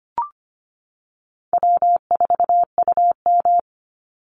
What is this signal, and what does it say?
Morse code, heard as a keyed tone, about 25 wpm, 700 Hz. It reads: W4UM
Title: Easy Call Signs - 25wpm